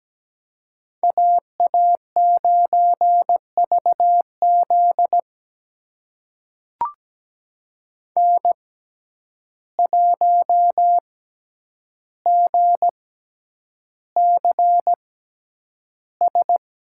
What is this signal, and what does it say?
Morse code, heard as a keyed tone, about 17 wpm, spaced as for 9 wpm, 700 Hz. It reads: AA9VZ N1GCS
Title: USA Call Signs - 6x Character Spacing - 17wpm